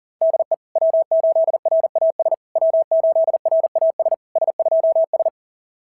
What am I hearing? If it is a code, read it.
Morse code, heard as a keyed tone, about 37 wpm, 660 Hz. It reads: DE W8RAS W8RAS S2S
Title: SOTA QSO Practice 40wpm